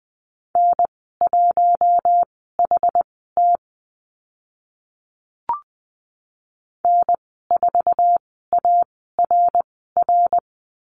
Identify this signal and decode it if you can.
Morse code, heard as a keyed tone, about 20 wpm, spaced as for 14 wpm, 700 Hz. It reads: N1HT N4ARR
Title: 062 Lesson 11g - Letter h - Call Signs 20wpm